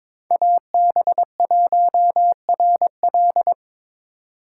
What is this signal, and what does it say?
Morse code, heard as a keyed tone, about 25 wpm, 700 Hz. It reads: AB1RL
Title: Speed Racing - USA Call Signs 22wpm